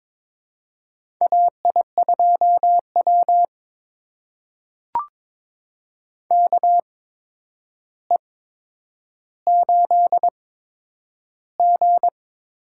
Morse code, heard as a keyed tone, about 22 wpm, spaced as for 9 wpm, 700 Hz. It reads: AI2W KE8G
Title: USA Call Signs - 4 Characters - 8x Character Spacing - 22wpm